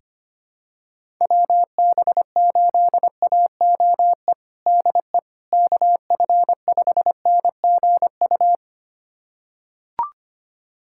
Morse code, heard as a keyed tone, about 25 wpm, 700 Hz. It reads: WB8AOE DE KF5NGU
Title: USA Callsign DE Callsign - 6 Characters - 25wpm